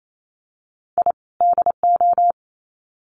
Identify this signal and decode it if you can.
Morse code, heard as a keyed tone, about 28 wpm, 700 Hz. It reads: I DO
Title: Sets of 2 Words 28wpm